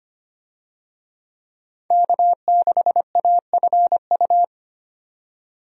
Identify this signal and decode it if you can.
Morse code, heard as a keyed tone, about 25 wpm, 700 Hz. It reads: K6AFU
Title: USA Call Signs - 5 Characters - 25wpm